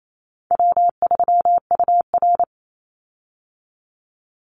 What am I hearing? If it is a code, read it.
Morse code, heard as a keyed tone, about 28 wpm, 700 Hz. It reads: W3UR
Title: Easy Call Signs - 28wpm